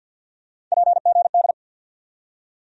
Morse code, heard as a keyed tone, about 47 wpm, 700 Hz. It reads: PGD
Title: USA Call Sign Postfixes - 3 Letters 50wpm